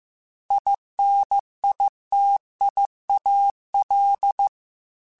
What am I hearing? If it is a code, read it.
Morse code, heard as a keyed tone, about 15 wpm, 800 Hz. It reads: INITIAL
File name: Technical Subjects / Technical Subjects-mp3s / Gates Intro @13